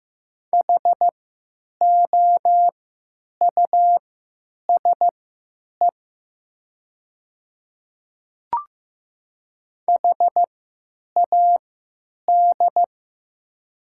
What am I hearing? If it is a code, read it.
Morse code, heard as a keyed tone, about 15 wpm, spaced as for 9 wpm, 700 Hz. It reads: HOUSE HAD
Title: 3x Character Spacing - Top 300 Words - 15wpm